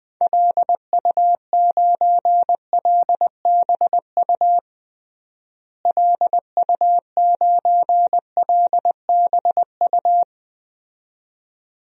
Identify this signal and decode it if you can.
Morse code, heard as a keyed tone, about 20 wpm, 700 Hz. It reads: LU9LBU LU9LBU
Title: DXCC-100 Call Signs - 6 Characters - Repeated 2x - 20wpm